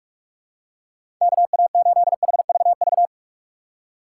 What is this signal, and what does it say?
Morse code, heard as a keyed tone, about 45 wpm, 700 Hz. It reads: KA8HVV
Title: USA Call Signs - 6 Characters - 3x Character Spacing - 45wpm